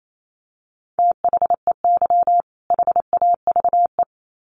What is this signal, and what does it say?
Morse code, heard as a keyed tone, about 28 wpm, 700 Hz. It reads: THEY HAVE
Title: Sets of 2 Words 28wpm